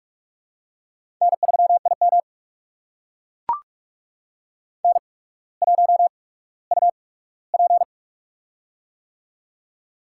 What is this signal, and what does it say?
Morse code, heard as a keyed tone, about 44 wpm, spaced as for 19 wpm, 700 Hz. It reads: N3IM N1UP
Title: USA Call Signs - 4 Characters - 8x Character Spacing - 45wpm